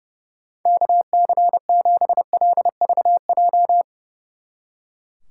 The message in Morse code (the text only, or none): KC7LVJ